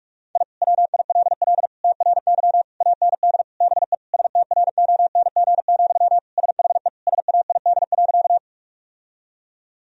Morse code, heard as a keyed tone, about 46 wpm, 700 Hz. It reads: I WILL TRY AND BE STRONG, SHE SAID.
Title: Sentences from Top 1000 Words - Review Part 3 - 45wpm